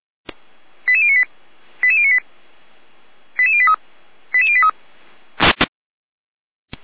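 Five-tone selective call (ZVEI), heard as quick runs of five tones, tones of about 70 ms. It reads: 88098 88098 88083 88083
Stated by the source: SDR-received ZVEI five-tone selective call.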